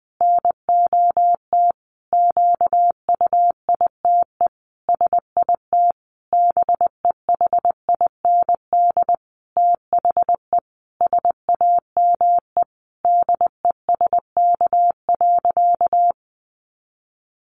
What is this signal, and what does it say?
Morse code, heard as a keyed tone, about 20 wpm, 700 Hz. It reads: NOT QUITE SIT BEHIND THE SAME DESK.
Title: A Sound Of Thunder 20wpm